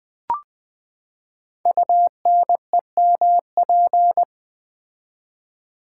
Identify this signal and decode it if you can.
Morse code, heard as a keyed tone, about 20 wpm, 700 Hz. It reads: UNEMP
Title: Build A Word - 5 Letters At A Time - 12-Letter Words - 20wpm